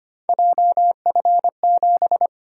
Morse code, heard as a keyed tone, about 25 wpm, 700 Hz. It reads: JF7
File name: Three Letter-Number Combinations - 25wpm